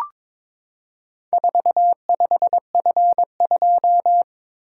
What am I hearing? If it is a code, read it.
Morse code, heard as a keyed tone, about 22 wpm, 690 Hz. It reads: 45F2
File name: practice / 22wpm / Four Letter-Number Combinations - 22wpm